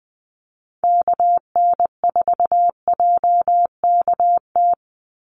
Morse code, heard as a keyed tone, about 20 wpm, 700 Hz. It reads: KN4JKT
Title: USA Call Signs 20wpm